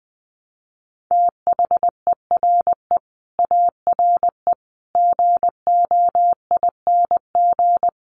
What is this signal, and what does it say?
Morse code, heard as a keyed tone, about 19 wpm, 700 Hz. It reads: THERE ARE GOING
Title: Sets of 5 Words 20wpm